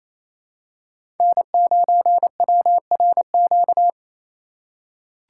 Morse code, heard as a keyed tone, about 28 wpm, 700 Hz. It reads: N9WRQ
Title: USA Call Signs - 5 Characters - 2x Character Spacing - 28wpm